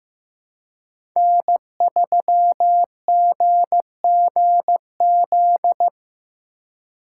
Morse code, heard as a keyed tone, about 18 wpm, 700 Hz. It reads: N3GGZ
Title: Speed Racing - USA 1x3 Call Signs 15wpm